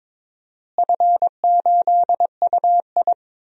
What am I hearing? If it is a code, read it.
Morse code, heard as a keyed tone, about 22 wpm, 700 Hz. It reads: F8UI
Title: DXCC-100 Call Signs - 4 Characters - Repeated 2x - 22wpm